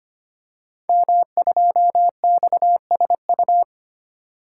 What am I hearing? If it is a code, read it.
Morse code, heard as a keyed tone, about 25 wpm, 700 Hz. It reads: M2XSU
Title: Code Groups - 5 Letter:Numbers - 25wpm